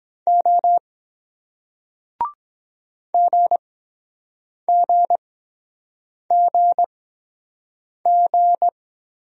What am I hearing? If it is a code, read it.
Morse code, heard as a keyed tone, about 23 wpm, spaced as for 7 wpm, 700 Hz. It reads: O GGGG